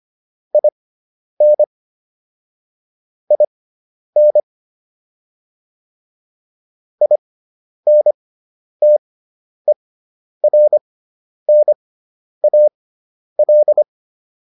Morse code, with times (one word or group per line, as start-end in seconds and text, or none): 0.5-1.6 IN
3.3-4.4 IN
7.0-13.8 INTERNAL